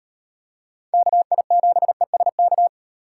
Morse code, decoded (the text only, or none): KI7ESK